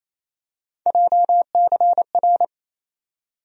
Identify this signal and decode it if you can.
Morse code, heard as a keyed tone, about 28 wpm, 700 Hz. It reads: JCR
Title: USA Call Sign Postfixes - 3 Letters 28wpm